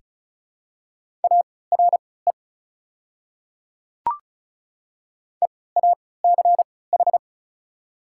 Morse code, heard as a keyed tone, about 36 wpm, spaced as for 20 wpm, 700 Hz. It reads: ARE EACH
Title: 3x Character Spacing - Top 100 Words - 35wpm